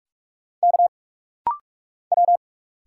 Morse code, heard as a keyed tone, about 43 wpm, 700 Hz. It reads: K W
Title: Single Letter-Number Alt - Rapid-Fire 45wpm